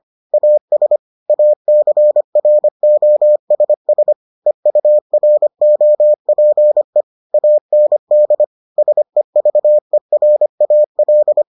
Morse code, heard as a keyed tone, about 25 wpm, 600 Hz. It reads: AS ACROSS EUROPE AND SEVERAL